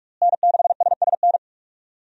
Morse code, heard as a keyed tone, about 46 wpm, 700 Hz. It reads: N6SSN
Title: USA Call Signs - 5 Characters - Repeated 3x - 45wpm